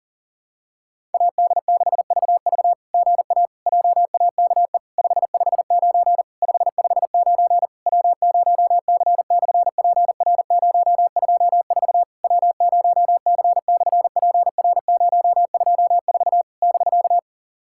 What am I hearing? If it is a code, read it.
Morse code, heard as a keyed tone, about 41 wpm, 700 Hz. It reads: AD6VV GA JAKE 559 559 W0C/PR024 W0C/PR024 <BK>
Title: SOTA QSO Practice 40wpm